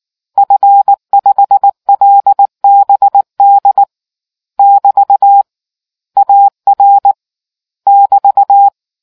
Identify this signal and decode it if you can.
Morse code, heard as a keyed tone, about 19 wpm, 800 Hz. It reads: F5LBD <BT> AR <BT>